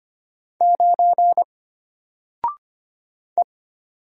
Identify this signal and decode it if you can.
Morse code, heard as a keyed tone, about 25 wpm, 700 Hz. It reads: 9 E
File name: Single-Letter-Number-25wpm